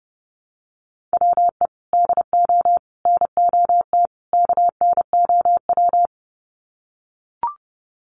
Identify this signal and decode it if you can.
Morse code, heard as a keyed tone, about 30 wpm, 700 Hz. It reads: WE DO NOT KNOW
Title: Sets of 4 Words 30wpm